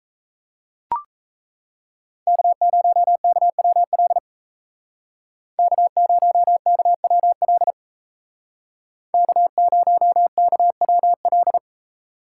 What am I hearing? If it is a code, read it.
Morse code, heard as a keyed tone, about 36 wpm, 700 Hz. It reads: K0KWL K0KWL K0KWL